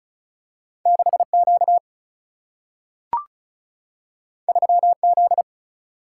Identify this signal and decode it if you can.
Morse code, heard as a keyed tone, about 36 wpm, 700 Hz. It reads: 6Q 3Z
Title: Single-Letter Single-Number 35wpm